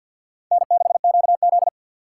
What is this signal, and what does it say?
Morse code, heard as a keyed tone, about 62 wpm, 700 Hz. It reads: N6QZ
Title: Speed Racing - USA 1x2 Call Signs 50wpm